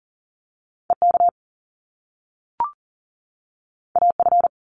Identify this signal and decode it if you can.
Morse code, heard as a keyed tone, about 40 wpm, 700 Hz. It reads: EK AF
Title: Top Two Letter Combinations in Words 40wpm